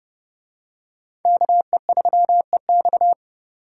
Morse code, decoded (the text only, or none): KE3EX